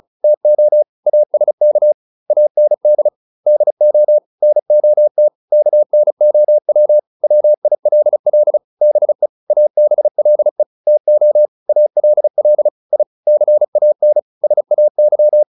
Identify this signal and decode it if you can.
Morse code, heard as a keyed tone, about 34 wpm, 600 Hz. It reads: TO ASK AND DO NOT KNOW WILL BE ABLE TO ALL I CAN SAY